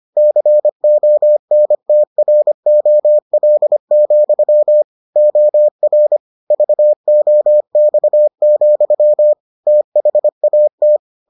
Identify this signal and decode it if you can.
Morse code, heard as a keyed tone, about 25 wpm, 600 Hz. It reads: CONTROL, OR VOX, THAT